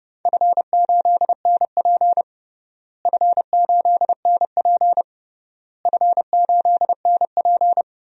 Morse code, heard as a keyed tone, about 31 wpm, 700 Hz. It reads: F8NP F8NP F8NP
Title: DXCC-100 Call Signs - 4 Characters - Repeated 3x - 30wpm